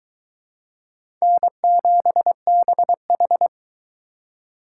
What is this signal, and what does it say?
Morse code, heard as a keyed tone, about 22 wpm, 700 Hz. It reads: N7BH